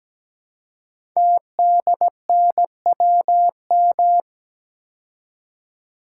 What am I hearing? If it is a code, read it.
Morse code, heard as a keyed tone, about 17 wpm, 700 Hz. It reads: TDNWM
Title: Code Groups - 5 Letters - 17wpm